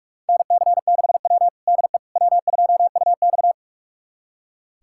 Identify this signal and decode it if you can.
Morse code, heard as a keyed tone, about 41 wpm, 700 Hz. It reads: NC6W DE W2UX
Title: USA Callsign DE Callsign - 4 Characters - 45wpm